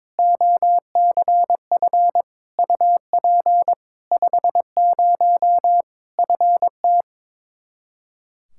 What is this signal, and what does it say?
Morse code, heard as a keyed tone, about 22 wpm, 700 Hz. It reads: OCF UP 50 FT
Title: Practice QSO - Exchange 2 - Info - 22wpm